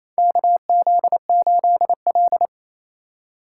Morse code, heard as a keyed tone, about 27 wpm, 700 Hz. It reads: KZ8L
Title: USA Call Signs - 4x Character Spacing - 28wpm